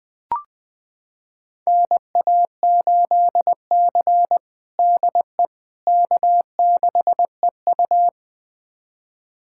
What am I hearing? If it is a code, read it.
Morse code, heard as a keyed tone, about 20 wpm, 700 Hz. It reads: NA8C DE K6EU
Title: USA Callsign DE Callsign - 4 Characters - 20wpm